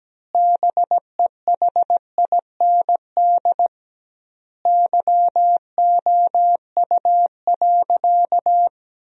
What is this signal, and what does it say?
Morse code, heard as a keyed tone, about 17 wpm, 700 Hz. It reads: BEHIND YOU.
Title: 2x Word-Spacing - Sentences from Top 500 Words - 17wpm